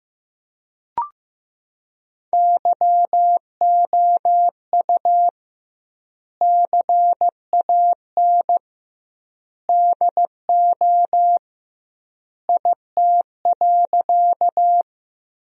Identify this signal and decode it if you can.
Morse code, heard as a keyed tone, about 15 wpm, 700 Hz. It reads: YOU CAN DO IT.
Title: 2x Word-Spacing - Sentences from Top 100 Words - 15wpm